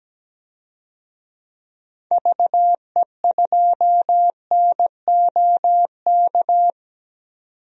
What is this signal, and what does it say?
Morse code, heard as a keyed tone, about 17 wpm, 700 Hz. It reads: VE2NOK
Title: DXCC-100 Call Signs - 6 Characters - 17wpm